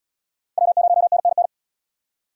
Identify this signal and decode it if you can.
Morse code, heard as a keyed tone, about 73 wpm, 700 Hz. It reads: W0NNN